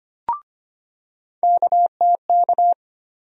Morse code, heard as a keyed tone, about 25 wpm, 700 Hz. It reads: KTK
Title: USA Call Sign Postfixes - 3 Letters 25wpm